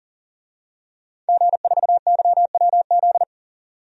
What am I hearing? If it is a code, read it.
Morse code, heard as a keyed tone, about 40 wpm, 700 Hz. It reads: G4YWZ
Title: DXCC-100 Call Signs - 5 Characters - 40wpm